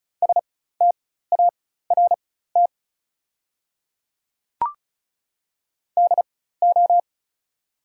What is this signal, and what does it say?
Morse code, heard as a keyed tone, about 36 wpm, spaced as for 16 wpm, 690 Hz. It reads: START DO
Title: 4x Character Spacing - Top 500 Words - 35wpm